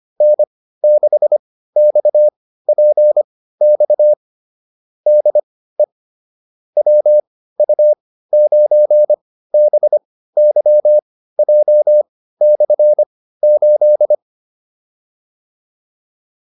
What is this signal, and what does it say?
Morse code, heard as a keyed tone, about 25 wpm, spaced as for 15 wpm, 600 Hz. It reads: N6XPX DE WU9BYJ/8